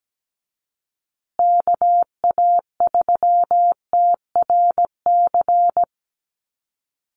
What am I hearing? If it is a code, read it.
Morse code, heard as a keyed tone, about 17 wpm, 700 Hz. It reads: KA3TRC
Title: USA Call Signs Repeated 3x 17wpm